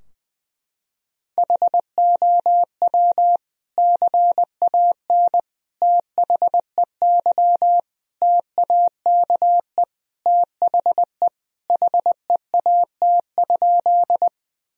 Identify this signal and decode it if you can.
Morse code, heard as a keyed tone, about 20 wpm, 700 Hz. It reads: HOW CAN THEY TAKE THE HEAT?
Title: Sentences from Top 1000 Words - Review Part 3 - 20wpm